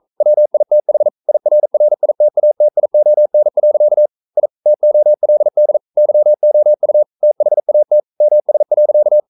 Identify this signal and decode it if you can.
Morse code, heard as a keyed tone, about 42 wpm, 600 Hz. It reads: WITH IRRITATION. I TOLD YOU THAT MS.
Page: Romanen Never af Ken Follet på engelsk.